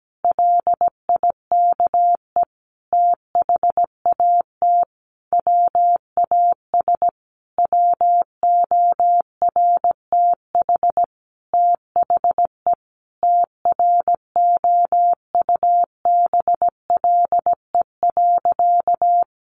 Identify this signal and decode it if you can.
Morse code, heard as a keyed tone, about 17 wpm, 700 Hz. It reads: LIKE THAT WAS WORTH THE TROUBLE.
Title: Sentences from Top 1500 Words 17wpm